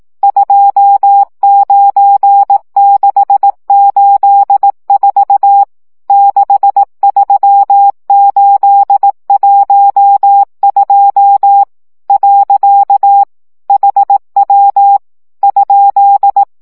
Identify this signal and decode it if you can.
Morse code, heard as a keyed tone, about 18 wpm, 800 Hz. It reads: 29684 63812 . HW ?
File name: morze vizsga 2011-1 15